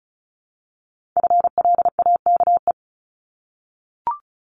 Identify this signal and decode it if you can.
Morse code, heard as a keyed tone, about 36 wpm, 700 Hz. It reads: FLAKE
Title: Multi-letter Phonemes 35wpm